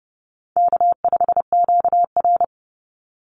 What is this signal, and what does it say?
Morse code, heard as a keyed tone, about 30 wpm, 700 Hz. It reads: K5QR
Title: NA SOTA Chaser Callsigns 30wpm